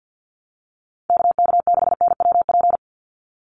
Morse code, heard as a keyed tone, about 50 wpm, 700 Hz. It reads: KK6NWP
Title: USA Call Signs 50wpm